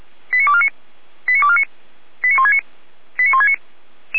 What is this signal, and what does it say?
Five-tone selective call (ZVEI), heard as quick runs of five tones, tones of about 70 ms. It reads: 89249 89249 89169 89169